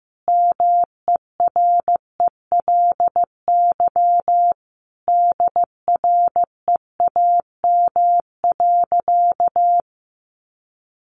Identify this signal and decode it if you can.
Morse code, heard as a keyed tone, about 14 wpm, 700 Hz. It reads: MERELY DREAM.
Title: Sentences from Top 1500 Words 15wpm